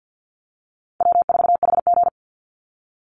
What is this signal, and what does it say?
Morse code, heard as a keyed tone, about 48 wpm, 700 Hz. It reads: W4HG
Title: Easy Call Signs Repeated 3x - 50wpm